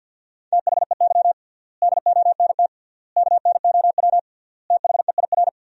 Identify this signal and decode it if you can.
Morse code, heard as a keyed tone, about 50 wpm, 700 Hz. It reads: THEY DONT KNOW THEIR
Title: 3x Word-Spacing - Sentences from Top 300 Words - 50wpm